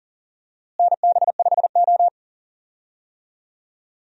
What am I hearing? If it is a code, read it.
Morse code, heard as a keyed tone, about 40 wpm, 700 Hz. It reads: NB5O
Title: USA Call Signs - 4 Characters - Repeated 2x - 40wpm